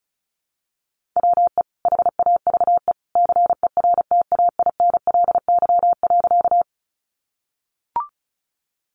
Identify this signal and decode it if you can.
Morse code, heard as a keyed tone, about 34 wpm, 700 Hz. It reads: WE HAVE CERTAINLY.
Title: Sentences from Top 900 Words 35wpm